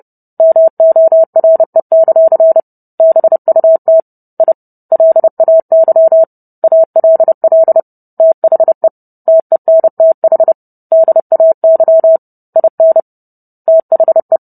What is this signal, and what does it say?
Morse code, heard as a keyed tone, about 30 wpm, 650 Hz. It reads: MORE; BUT I LAY ALL THE TENTH DAY IN THE